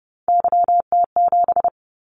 Morse code, heard as a keyed tone, about 32 wpm, 700 Hz. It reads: YT7
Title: Callsign Prefixes for Top 25 DXCC - 30wpm